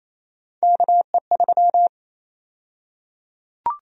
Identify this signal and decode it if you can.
Morse code, heard as a keyed tone, about 27 wpm, 700 Hz. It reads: KE3
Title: USA Call Sign Prefixes - Repeated 2x - 28wpm